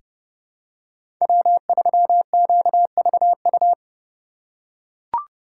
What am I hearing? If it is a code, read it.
Morse code, heard as a keyed tone, about 31 wpm, 700 Hz. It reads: W3QVU
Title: USA Call Signs - 5 Characters - 3x Character Spacing - 30wpm